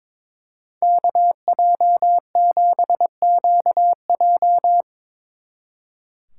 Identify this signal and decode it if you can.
Morse code, heard as a keyed tone, about 25 wpm, 700 Hz. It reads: KJ7QJ
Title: Speed Racing - USA Call Signs 22wpm